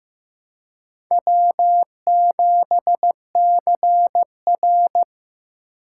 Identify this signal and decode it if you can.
Morse code, heard as a keyed tone, about 18 wpm, 700 Hz. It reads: W7CR
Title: Speed Racing - USA 1x2 Call Signs 15wpm